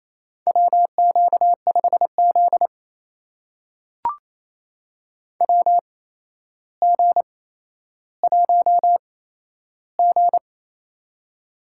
Morse code, heard as a keyed tone, about 28 wpm, spaced as for 12 wpm, 700 Hz. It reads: WQ5Z WG1G
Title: USA Call Signs - 4 Characters - 8x Character Spacing - 28wpm